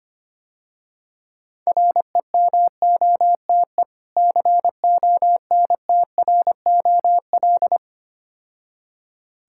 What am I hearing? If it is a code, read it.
Morse code, heard as a keyed tone, about 25 wpm, 700 Hz. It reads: REMOTE CONTROL
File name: Easy Words - Speed-Racing with Hints - 25wpm